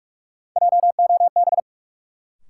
Morse code, heard as a keyed tone, about 46 wpm, 700 Hz. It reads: JOB
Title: Sentences from Top 600 Words - New Word Drill - 45wpm